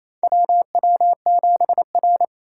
Morse code, heard as a keyed tone, about 28 wpm, 700 Hz. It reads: WW7R
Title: USA Call Signs - 4 Characters - 4x Character Spacing - 28wpm